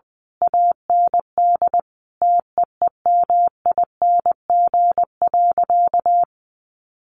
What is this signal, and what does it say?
Morse code, heard as a keyed tone, about 19 wpm, 700 Hz. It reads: AND TEEMING.
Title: A Sound Of Thunder 20wpm